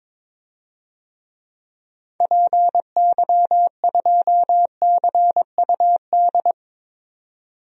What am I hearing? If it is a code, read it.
Morse code, heard as a keyed tone, about 22 wpm, 700 Hz. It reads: PY2CUD
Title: DXCC-100 Call Signs - 6 Characters - 22wpm